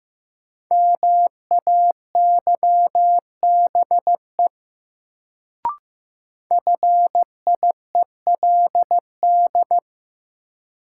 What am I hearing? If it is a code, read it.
Morse code, heard as a keyed tone, about 15 wpm, 700 Hz. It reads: MAYBE FIELD
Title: Sentences from Top 1000 Words - New Word Drill - 15wpm